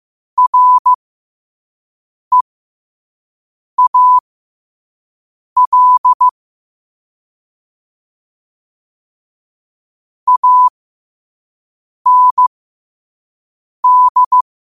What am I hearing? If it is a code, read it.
Morse code, heard as a keyed tone, about 15 wpm, spaced as for 5 wpm, 1000 Hz. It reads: REAL AND
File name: qotdfiles / 5wpm